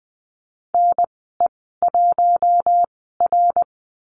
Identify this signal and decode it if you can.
Morse code, heard as a keyed tone, about 20 wpm, spaced as for 15 wpm, 700 Hz. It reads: NE1R
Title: 055 Lesson 10g - Letter r - Call Signs 20wpm